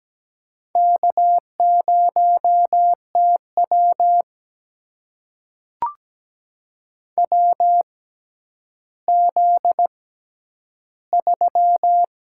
Morse code, heard as a keyed tone, about 17 wpm, spaced as for 10 wpm, 700 Hz. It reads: K0TW WZ3
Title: USA Call Signs - 4 Characters - 6x Character Spacing - 17wpm